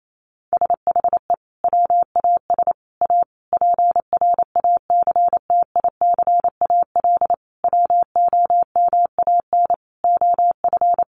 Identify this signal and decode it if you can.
Morse code, heard as a keyed tone, about 28 wpm, 700 Hz. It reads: SHE WAS A PRACTICAL WOMAN OF
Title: Sentences from Top 1400 Words 28wpm